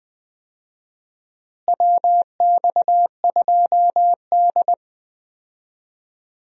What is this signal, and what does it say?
Morse code, heard as a keyed tone, about 20 wpm, 700 Hz. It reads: WX2D
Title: USA Call Signs - 4 Characters - Repeated 2x - 20wpm